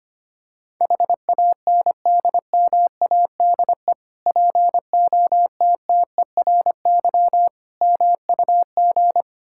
Morse code, heard as a keyed tone, about 25 wpm, 700 Hz. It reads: HANDMADE POTTERY MUG
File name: practice / 25wpm / Difficult Words with Hints - 25wpm